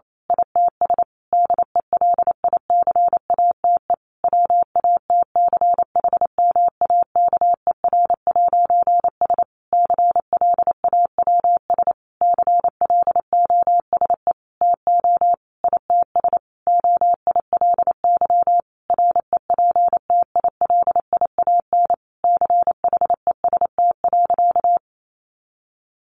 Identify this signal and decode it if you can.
Morse code, heard as a keyed tone, about 28 wpm, 700 Hz. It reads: ITS DELICATE WATCHMAKER'S CLAWS CLOSE TO ITS OILY REPTILIAN CHEST.
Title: A Sound Of Thunder 28wpm